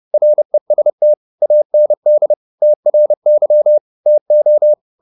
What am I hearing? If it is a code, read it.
Morse code, heard as a keyed tone, about 30 wpm, 600 Hz. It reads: REST AND TRY TO